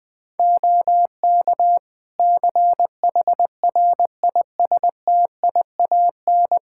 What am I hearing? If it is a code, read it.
Morse code, heard as a keyed tone, about 20 wpm, 700 Hz. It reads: OK CHRISTIAN
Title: Practice QSO - Exchange 3 - More Info - 20wpm